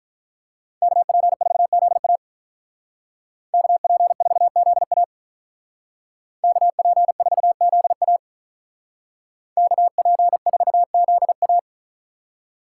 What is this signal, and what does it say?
Morse code, heard as a keyed tone, about 43 wpm, 700 Hz. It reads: KP4ZA KP4ZA KP4ZA KP4ZA
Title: Speed Racing - USA 2x2 Call Signs 35wpm